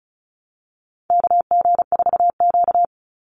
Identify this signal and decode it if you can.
Morse code, heard as a keyed tone, about 34 wpm, 700 Hz. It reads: KG4Q
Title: Call Signs - 35wpm